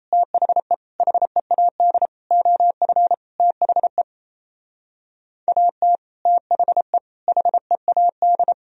Morse code, heard as a keyed tone, about 33 wpm, 700 Hz. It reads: THE HEAD OF THE AT THE HEAD
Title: Speed-Racing - Sets of 5 Words 28wpm